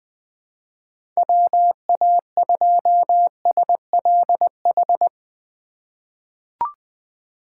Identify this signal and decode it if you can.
Morse code, heard as a keyed tone, about 21 wpm, 700 Hz. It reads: WA2SLH
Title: 2023 POTA Hunter Call Signs - 20wpm